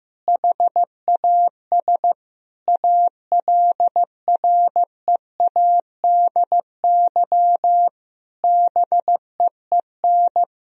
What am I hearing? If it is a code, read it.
Morse code, heard as a keyed tone, about 15 wpm, 700 Hz. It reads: HAS ALREADY BEEN
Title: Sentences from Top 1000 Words - Review Part 1 - 15wpm